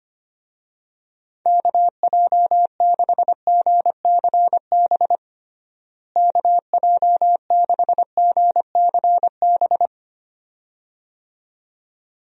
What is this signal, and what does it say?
Morse code, heard as a keyed tone, about 25 wpm, 700 Hz. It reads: KJ6GCB KJ6GCB
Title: USA Call Signs - 6 Characters - Repeated 2x - 25wpm